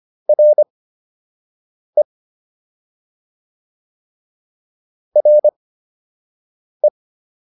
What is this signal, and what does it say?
Morse code, heard as a keyed tone, about 25 wpm, spaced as for 6 wpm, 600 Hz. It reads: RE RE